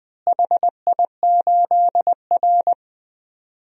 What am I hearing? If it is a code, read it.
Morse code, heard as a keyed tone, about 20 wpm, 700 Hz. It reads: HI8R
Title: DXCC-100 Call Signs - 4 Characters - Repeated 2x - 20wpm